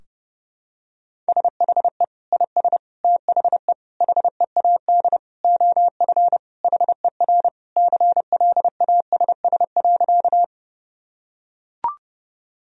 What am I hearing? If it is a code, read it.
Morse code, heard as a keyed tone, about 32 wpm, 700 Hz. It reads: SHE IS THE HEAD OF HER CLASS.
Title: Sentences from Top 1000 Words - Review Part 2 - 30wpm